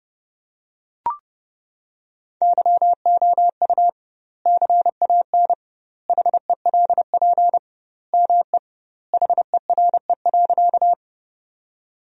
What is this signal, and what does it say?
Morse code, heard as a keyed tone, about 32 wpm, 700 Hz. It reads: YOU CAN HELP ME HERE.
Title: 2x Word-Spacing - Sentences from Top 200 Words - 30wpm